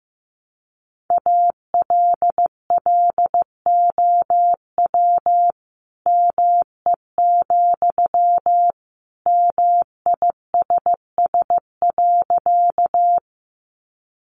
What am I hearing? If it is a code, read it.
Morse code, heard as a keyed tone, about 15 wpm, 700 Hz. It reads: ALLOW ME, MISS.
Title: Sentences from Top 700 Words 15wpm